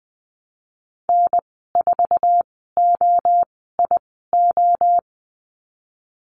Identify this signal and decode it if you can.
Morse code, heard as a keyed tone, about 20 wpm, spaced as for 15 wpm, 700 Hz. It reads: N4OIO
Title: 055 Lesson 10g - Letter r - Call Signs 20wpm